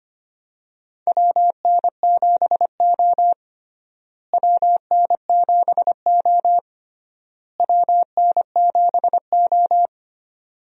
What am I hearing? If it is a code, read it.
Morse code, heard as a keyed tone, about 25 wpm, 700 Hz. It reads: WN7O WN7O WN7O
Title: USA Call Signs - 4 Characters - Repeated 3x - 25wpm